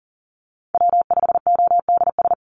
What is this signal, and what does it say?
Morse code, heard as a keyed tone, about 40 wpm, 700 Hz. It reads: W5ODS
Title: NA SOTA Chaser Callsigns 40wpm